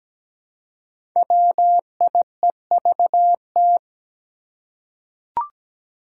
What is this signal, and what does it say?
Morse code, heard as a keyed tone, about 17 wpm, 700 Hz. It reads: WIEVT
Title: Five Challenging Letter-Number Combinations 17wpm